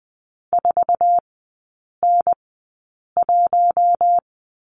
Morse code, heard as a keyed tone, about 20 wpm, spaced as for 8 wpm, 700 Hz. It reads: 4N1
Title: 047 Lesson 9e - Number 4 - Three Characters 20wpm